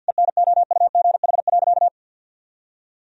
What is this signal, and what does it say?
Morse code, heard as a keyed tone, about 50 wpm, 700 Hz. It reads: ENOUGH.
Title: Sentences from Top 1000 Words - Review Part 2 - 50wpm